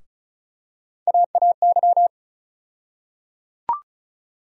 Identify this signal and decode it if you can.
Morse code, heard as a keyed tone, about 36 wpm, 700 Hz. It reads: AAY
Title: USA Call Sign Postfixes - 3 Letters 35wpm